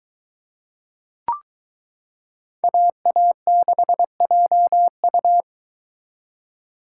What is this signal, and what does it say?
Morse code, heard as a keyed tone, about 22 wpm, 700 Hz. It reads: AA6JU